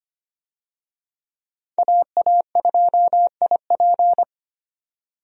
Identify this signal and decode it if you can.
Morse code, heard as a keyed tone, about 25 wpm, 700 Hz. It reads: AA2IP
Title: USA Call Signs - 5 Characters - 25wpm